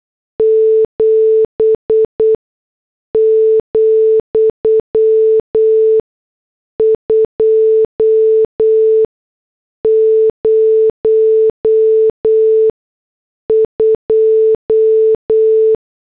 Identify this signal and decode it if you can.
Morse code, heard as a keyed tone, about 8 wpm, spaced as for 6 wpm, 440 Hz. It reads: 7, 202
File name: morse / headlines